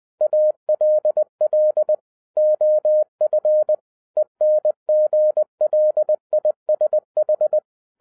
Morse code, heard as a keyed tone, about 20 wpm, 610 Hz. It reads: ALL OF ENGLISH